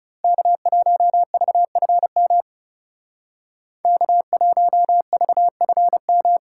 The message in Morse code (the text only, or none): K1VFM K1VFM